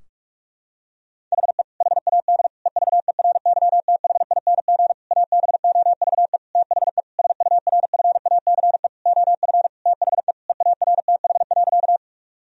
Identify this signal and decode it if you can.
Morse code, heard as a keyed tone, about 41 wpm, 700 Hz. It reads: HE HAD EVERYTHING ABOVE THE SURFACE OF THE EARTH.
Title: Sentences from Top 1000 Words - Review Part 3 - 45wpm